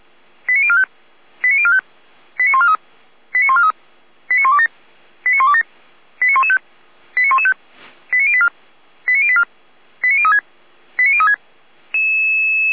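Five-tone selective call (ZVEI), heard as quick runs of five tones, tones of about 70 ms. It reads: 89045 89045 89143 89143 89127 89127 89115 89115 89084 89084 89036 89036
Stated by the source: ZVEI five-tone selective call received on SDR.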